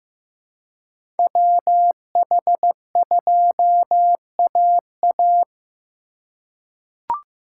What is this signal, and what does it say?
Morse code, heard as a keyed tone, about 15 wpm, 700 Hz. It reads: WH2AA
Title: USA Call Signs - 5 Characters - 2x Character Spacing - 15wpm